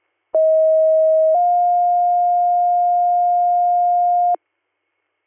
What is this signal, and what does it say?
Two-tone page, heard as a one-second tone, then a three-second tone, about 640 then 710 Hz. It is heard from 0.3 to 4.4 s.